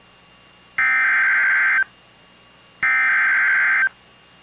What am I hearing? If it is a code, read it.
Emergency Alert System (SAME) tones (header bursts), heard as harsh buzzing bursts. It is heard from 0.8 to 3.8 s.